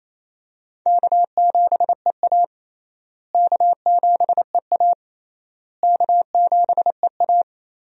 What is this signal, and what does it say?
Morse code, heard as a keyed tone, about 28 wpm, 700 Hz. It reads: K7EA K7EA K7EA
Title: USA Call Signs - 4 Characters - Repeated 3x - 28wpm